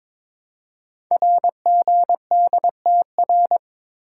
Morse code, heard as a keyed tone, about 22 wpm, 700 Hz. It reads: RGDTR
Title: Code Groups - 5 Letters - 22wpm